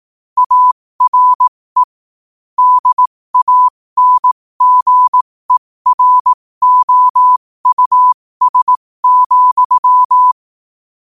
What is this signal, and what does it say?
Morse code, heard as a keyed tone, about 18 wpm, spaced as for 15 wpm, 1000 Hz. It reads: ARE DANGEROUS,
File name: qotdfiles / 15wpm